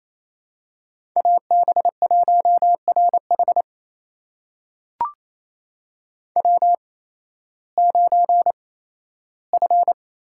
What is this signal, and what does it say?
Morse code, heard as a keyed tone, about 28 wpm, spaced as for 15 wpm, 700 Hz. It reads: AB1RH W9F
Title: USA Call Signs - 5 Characters - 8x Character Spacing - 28wpm